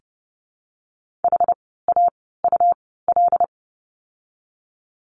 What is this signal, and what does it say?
Morse code, heard as a keyed tone, about 30 wpm, spaced as for 17 wpm, 700 Hz. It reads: HAUL
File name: Lesson 17f - Letter u - Words 30wpm